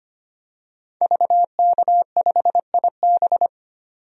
Morse code, heard as a keyed tone, about 25 wpm, 700 Hz. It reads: VK5IB
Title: DXCC-100 Call Signs - 5 Characters - 25wpm